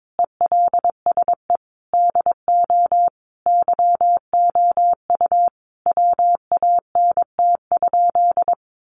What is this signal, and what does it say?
Morse code, heard as a keyed tone, about 22 wpm, 700 Hz. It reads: ELSE DO YOU WANT?
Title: Sentences from Top 800 Words 22wpm